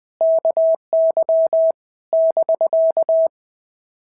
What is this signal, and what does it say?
Morse code, heard as a keyed tone, about 21 wpm, 650 Hz. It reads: KY <BK>
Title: POTA QSO Practice - 20wpm